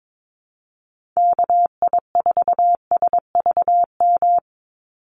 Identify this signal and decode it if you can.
Morse code, heard as a keyed tone, about 22 wpm, 700 Hz. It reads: KI4SVM
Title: NA SOTA Activator Callsigns 22wpm